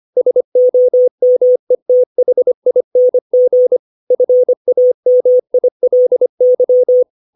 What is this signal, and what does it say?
Morse code, heard as a keyed tone, about 25 wpm, 500 Hz. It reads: SOMETHING FAMILY
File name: 632 words / 632 Words 25WPM